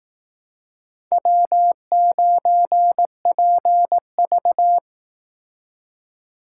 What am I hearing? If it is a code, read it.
Morse code, heard as a keyed tone, about 19 wpm, 700 Hz. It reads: W9PV